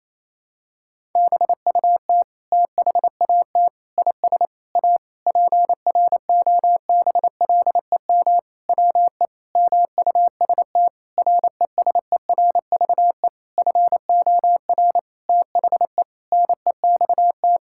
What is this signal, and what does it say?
Morse code, heard as a keyed tone, about 28 wpm, 700 Hz. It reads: BUT THAT IS A PROBLEM WE MUST RESERVE FOR THE NEXT
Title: Sentences from Top 1000 Words - Review Part 6 - 28wpm